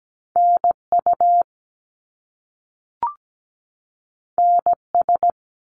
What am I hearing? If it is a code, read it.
Morse code, heard as a keyed tone, about 17 wpm, 700 Hz. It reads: NU NS
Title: Canadian Provinces and Territories 17wpm